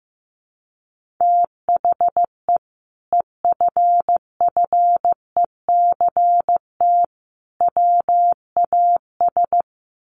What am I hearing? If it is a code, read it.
Morse code, heard as a keyed tone, about 15 wpm, 700 Hz. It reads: THE EFFECT WAS
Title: Sentences from Top 1200 Words 15wpm